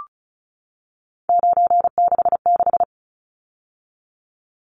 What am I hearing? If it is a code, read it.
Morse code, heard as a keyed tone, about 34 wpm, 700 Hz. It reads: 966
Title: Numbers - 3-Digit 35wpm